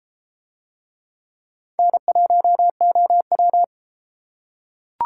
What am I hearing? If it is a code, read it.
Morse code, heard as a keyed tone, about 32 wpm, 700 Hz. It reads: N1OW